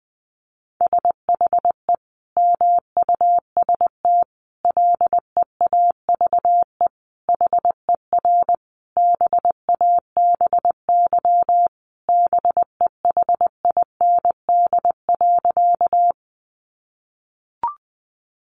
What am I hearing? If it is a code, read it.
Morse code, heard as a keyed tone, about 20 wpm, 700 Hz. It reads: SHE MUST LEAVE HER BABY BEHIND.
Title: Sentences from Top 1300 Words 20wpm